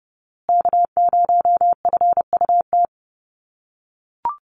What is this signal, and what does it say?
Morse code, heard as a keyed tone, about 30 wpm, 700 Hz. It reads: K0FUT
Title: USA Call Signs Repeated 3x 30wpm